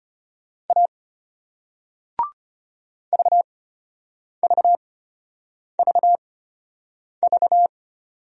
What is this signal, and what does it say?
Morse code, heard as a keyed tone, about 31 wpm, spaced as for 8 wpm, 700 Hz. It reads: A VVVV
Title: Speed-Racing Single Letters 25wpm